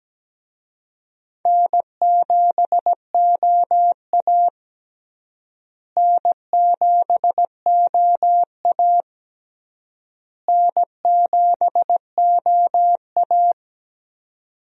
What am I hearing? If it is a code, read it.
Morse code, heard as a keyed tone, about 17 wpm, 700 Hz. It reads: N7OA N7OA N7OA
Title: USA Call Signs - 4 Characters - Repeated 3x - 17wpm